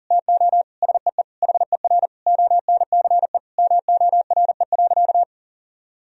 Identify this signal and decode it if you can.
Morse code, heard as a keyed tone, about 40 wpm, 700 Hz. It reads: TO SEE HER ONCE MORE.
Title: Sentences from Top 500 Words and Abbreviations 40wpm